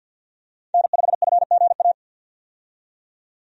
Morse code, heard as a keyed tone, about 50 wpm, 700 Hz. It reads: N5FGA
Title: USA Call Signs - 5 Characters - 50wpm